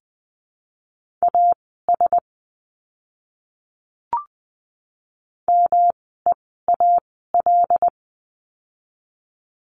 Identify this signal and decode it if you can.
Morse code, heard as a keyed tone, about 20 wpm, spaced as for 14 wpm, 700 Hz. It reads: AS MEAL
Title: 108 Lesson 18f - Letter m - Words 20wpm